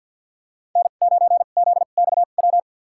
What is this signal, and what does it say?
Morse code, heard as a keyed tone, about 52 wpm, spaced as for 37 wpm, 700 Hz. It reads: N9ZXW
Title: USA Call Signs - 5 Characters - 2x Character Spacing - 50wpm